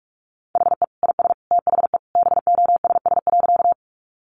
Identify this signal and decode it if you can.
Morse code, heard as a keyed tone, about 44 wpm, 700 Hz. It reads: HE IS THE BOSS.
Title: Sentences from Top 700 Words 45wpm